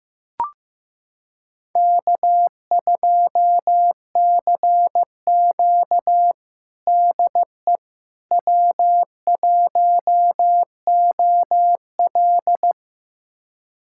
Morse code, heard as a keyed tone, about 15 wpm, 700 Hz. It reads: K2CQ DE W1OL
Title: USA Callsign DE Callsign - 4 Characters - 15wpm